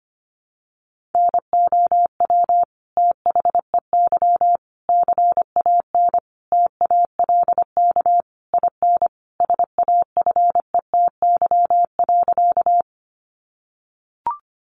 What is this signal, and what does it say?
Morse code, heard as a keyed tone, about 25 wpm, 700 Hz. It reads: NOW THEY CAN TALK IN SAFETY.
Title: Sentences from Top 700 Words 25wpm